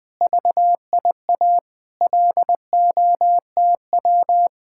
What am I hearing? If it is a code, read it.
Morse code, heard as a keyed tone, about 21 wpm, 700 Hz. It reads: VIA LOTW